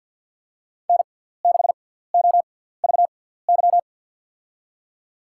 Morse code, heard as a keyed tone, about 52 wpm, spaced as for 17 wpm, 700 Hz. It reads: N6OVY
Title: USA Call Signs - 5 Characters - 6x Character Spacing - 50wpm